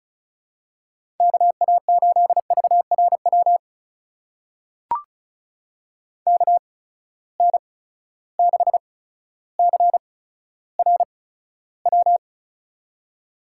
Morse code, heard as a keyed tone, about 36 wpm, spaced as for 15 wpm, 700 Hz. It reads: KA8VRW KN6CRW
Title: USA Call Signs - 6 Characters - 8x Character Spacing - 35wpm